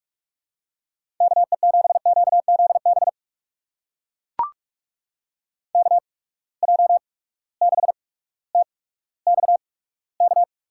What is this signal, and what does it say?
Morse code, heard as a keyed tone, about 44 wpm, spaced as for 19 wpm, 700 Hz. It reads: KE7QZB KJ6TXK
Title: USA Call Signs - 6 Characters - 8x Character Spacing - 45wpm